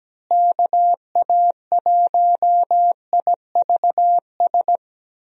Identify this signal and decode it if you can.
Morse code, heard as a keyed tone, about 17 wpm, 700 Hz. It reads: KA1IVS
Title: USA Call Signs - 6 Characters - Repeated 2x - 17wpm